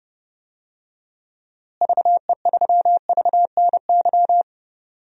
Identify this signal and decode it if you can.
Morse code, heard as a keyed tone, about 30 wpm, 700 Hz. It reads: VE3VNY
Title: DXCC-100 Call Signs - 6 Characters - Repeated 3x - 30wpm